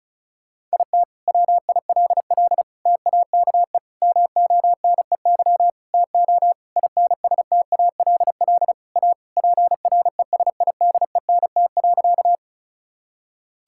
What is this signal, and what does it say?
Morse code, heard as a keyed tone, about 34 wpm, 700 Hz. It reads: IT WILL TAKE MONEY TO INSTALL A PRESIDENT.
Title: Sentences from Top 1600 Words 35wpm